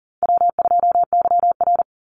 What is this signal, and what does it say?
Morse code, heard as a keyed tone, about 37 wpm, 700 Hz. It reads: W2YR
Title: Call Signs - 40wpm